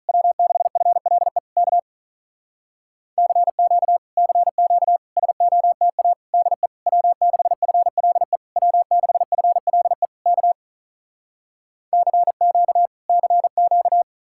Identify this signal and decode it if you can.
Morse code, heard as a keyed tone, about 46 wpm, 700 Hz. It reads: W6FLE K CQ CQ SOTA DE W6FLE W6FLE K CQ CQ